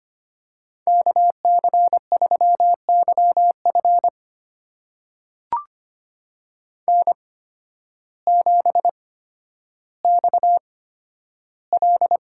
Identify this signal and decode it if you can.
Morse code, heard as a keyed tone, about 25 wpm, spaced as for 12 wpm, 700 Hz. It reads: KC3YF N7XL
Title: USA Call Signs - 5 Characters - 8x Character Spacing - 25wpm